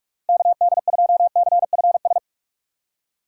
Morse code, heard as a keyed tone, about 55 wpm, 700 Hz. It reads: KD2CFS
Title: Speed Racing - USA 2x3 Call Signs 45wpm